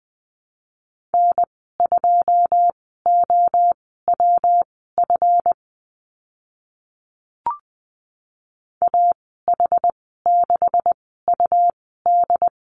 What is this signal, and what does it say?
Morse code, heard as a keyed tone, about 20 wpm, spaced as for 15 wpm, 700 Hz. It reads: N2OWF AH6UD
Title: 140 Lesson 23g - Letter f - Call Signs 20wpm